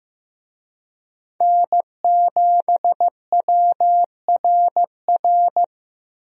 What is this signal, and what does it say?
Morse code, heard as a keyed tone, about 15 wpm, 700 Hz. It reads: N7WRR
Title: USA Call Signs - 5 Characters - 15wpm